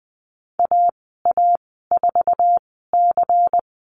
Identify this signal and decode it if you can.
Morse code, heard as a keyed tone, about 20 wpm, spaced as for 15 wpm, 700 Hz. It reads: AA4C
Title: 147 Lesson 24g - Letter y - Call Signs 20wpm